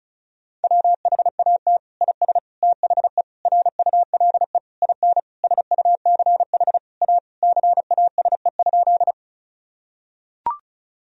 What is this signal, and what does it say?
Morse code, heard as a keyed tone, about 34 wpm, 700 Hz. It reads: WHAT IS THE RULE IN SUCH A CASE?
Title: Sentences from Top 1000 Words - Review Part 2 - 35wpm